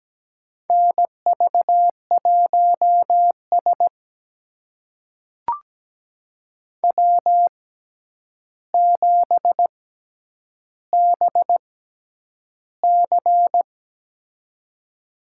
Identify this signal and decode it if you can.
Morse code, heard as a keyed tone, about 17 wpm, spaced as for 9 wpm, 700 Hz. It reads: NV1S W7BC
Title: USA Call Signs - 4 Characters - 6x Character Spacing - 17wpm